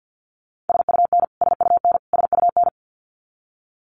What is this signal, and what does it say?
Morse code, heard as a keyed tone, about 49 wpm, 700 Hz. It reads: SUN SUN SUN
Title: Top 500 Words Repeated 50wpm